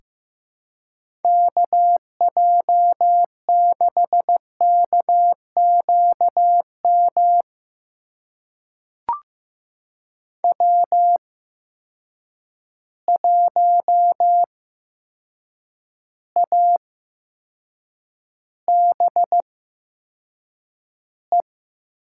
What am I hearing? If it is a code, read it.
Morse code, heard as a keyed tone, about 15 wpm, spaced as for 7 wpm, 700 Hz. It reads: KJ6KQM W1ABE
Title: USA Call Signs - 8x Character Spacing - 15wpm